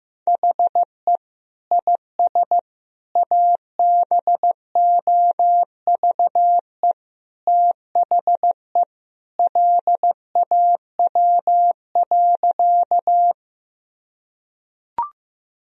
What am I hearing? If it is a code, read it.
Morse code, heard as a keyed tone, about 15 wpm, 700 Hz. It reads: HE IS ABOVE THE LAW.
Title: Sentences from Top 1000 Words - Review Part 3 - 15wpm